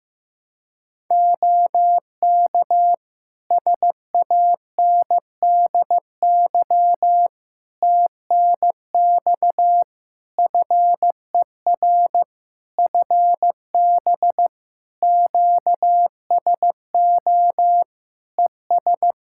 Morse code, heard as a keyed tone, about 15 wpm, 700 Hz. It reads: OK SANDY TNX FER FB QSO ES
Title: K1USN SST QSO Practice 15wpm